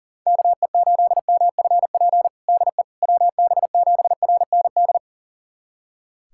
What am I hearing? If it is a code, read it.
Morse code, heard as a keyed tone, about 40 wpm, 700 Hz. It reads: KE8MFP DE WB7RND
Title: USA Callsign DE Callsign - 6 Characters - 40wpm